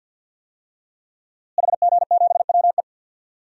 Morse code, heard as a keyed tone, about 47 wpm, 700 Hz. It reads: HG7WE
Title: DXCC-100 Call Signs - 5 Characters - Repeated 2x - 50wpm